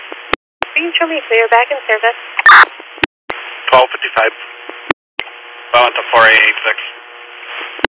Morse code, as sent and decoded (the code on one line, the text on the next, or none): none